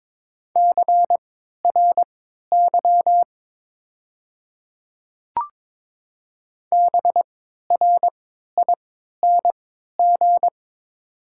Morse code, heard as a keyed tone, about 22 wpm, spaced as for 13 wpm, 700 Hz. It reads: CRY BRING
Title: 3x Character Spacing - Top 500 Words - 22wpm